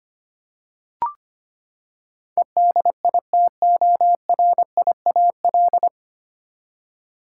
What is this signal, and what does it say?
Morse code, heard as a keyed tone, about 25 wpm, 700 Hz. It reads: EDITORIAL
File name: Nine Letter Words - 25wpm